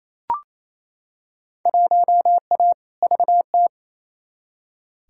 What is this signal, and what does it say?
Morse code, heard as a keyed tone, about 28 wpm, 700 Hz. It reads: 1A VT
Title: Field Day Exchanges Encore - 28wpm